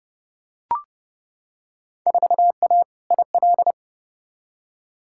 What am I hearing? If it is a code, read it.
Morse code, heard as a keyed tone, about 30 wpm, 700 Hz. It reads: 4A IL
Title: Field Day Exchanges - 30wpm